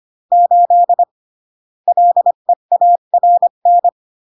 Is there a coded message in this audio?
Yes: Morse code from 0.3 to 3.9 s, heard as a keyed tone, about 25 wpm, 700 Hz. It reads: 8 LEARN